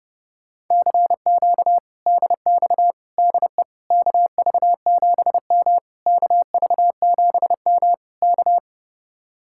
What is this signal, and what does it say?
Morse code, heard as a keyed tone, about 33 wpm, 700 Hz. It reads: CQ DX DE KV7M KV7M K
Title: Speed-Racing Common QSO Phrases - 30wpm